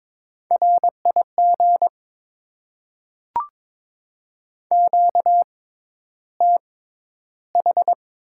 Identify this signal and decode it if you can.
Morse code, heard as a keyed tone, about 22 wpm, spaced as for 11 wpm, 690 Hz. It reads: RIG QTH
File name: Top 20 QSO Elements - 6x Character Spacing - 22wpm